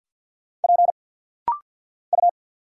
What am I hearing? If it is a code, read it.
Morse code, heard as a keyed tone, about 50 wpm, 700 Hz. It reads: P U
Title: Single Letter-Number - Rapid-Fire 50wpm